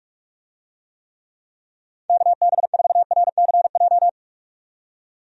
Morse code, heard as a keyed tone, about 47 wpm, 700 Hz. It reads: KB4RCJ
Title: USA Call Signs - 6 Characters - 45wpm